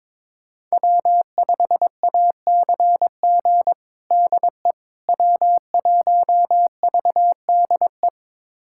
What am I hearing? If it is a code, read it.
Morse code, heard as a keyed tone, about 21 wpm, 700 Hz. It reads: W5ACG DE W1VDE
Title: USA Callsign DE Callsign - 5 Characters - 22wpm